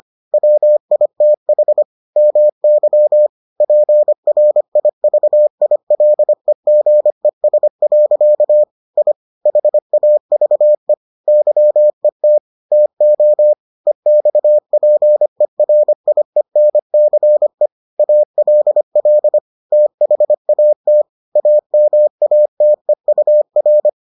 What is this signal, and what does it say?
Morse code, heard as a keyed tone, about 25 wpm, 600 Hz. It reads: WITH MY PRIVILEGES. I HAVE YET TO EXPERIENCE ALL THAT AMATEUR